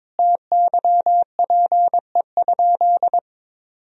Morse code, heard as a keyed tone, about 22 wpm, 700 Hz. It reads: TYPE?
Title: Sentences from Top 1000 Words - Review Part 3 - 22wpm